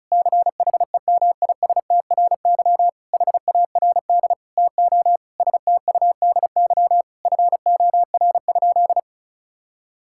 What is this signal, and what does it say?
Morse code, heard as a keyed tone, about 34 wpm, 700 Hz. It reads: CHEMISTRY HARD TO STUDY FOR?
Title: Sentences from Top 1000 Words - Review Part 6 - 35wpm